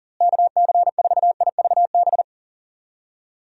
Morse code, heard as a keyed tone, about 40 wpm, 700 Hz. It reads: KC4IVB
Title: USA Call Signs - 6 Characters - Repeated 3x - 40wpm